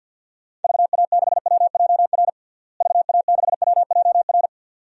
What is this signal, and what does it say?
Morse code, heard as a keyed tone, about 52 wpm, 700 Hz. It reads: VA6WJR VA6WJR
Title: DXCC-100 Call Signs - 6 Characters - Repeated 2x - 50wpm